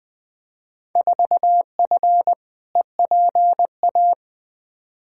Morse code, heard as a keyed tone, about 20 wpm, 700 Hz. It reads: 4F EPA
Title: Field Day Exchanges - 20wpm